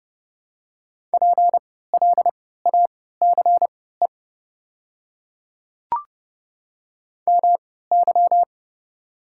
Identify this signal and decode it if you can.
Morse code, heard as a keyed tone, about 30 wpm, spaced as for 17 wpm, 700 Hz. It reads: PLACE MY
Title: 3x Character Spacing - Top 200 Words - 30wpm